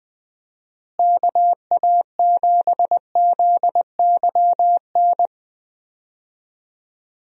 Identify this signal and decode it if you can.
Morse code, heard as a keyed tone, about 20 wpm, 700 Hz. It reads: KA7ZYN
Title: USA Call Signs - 6 Characters - Repeated 2x - 20wpm